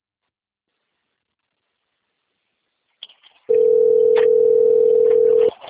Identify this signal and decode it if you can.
Ringback tone (North America), 440 + 480 Hz.
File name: out- - - - - .4708